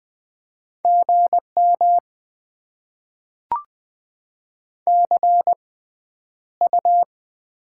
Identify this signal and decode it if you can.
Morse code, heard as a keyed tone, about 20 wpm, spaced as for 10 wpm, 690 Hz. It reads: GM CU
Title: Top 20 QSO Elements - 6x Character Spacing - 20wpm